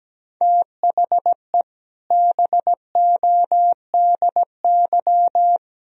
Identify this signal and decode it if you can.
Morse code, heard as a keyed tone, about 17 wpm, 700 Hz. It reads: THE BODY
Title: Sentences from Top 1000 Words - Review Part 2 - 17wpm